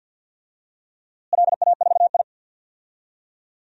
Morse code, heard as a keyed tone, about 52 wpm, 700 Hz. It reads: LA4I
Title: DXCC-100 Call Signs - 4 Characters - Repeated 3x - 50wpm